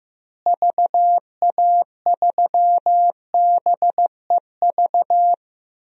Morse code, heard as a keyed tone, about 15 wpm, 700 Hz. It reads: VA3BEV
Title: DXCC-100 Call Signs - 6 Characters - Repeated 3x - 15wpm